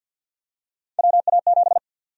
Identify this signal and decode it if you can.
Morse code, heard as a keyed tone, about 53 wpm, 700 Hz. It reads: WA7
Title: USA Call Sign Prefixes - 50wpm